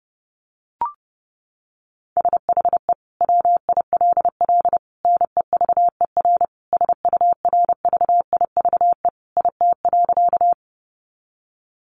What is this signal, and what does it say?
Morse code, heard as a keyed tone, about 30 wpm, 700 Hz. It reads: SHE WILL NEVER SURVIVE IT.
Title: Sentences from Top 1300 Words 30wpm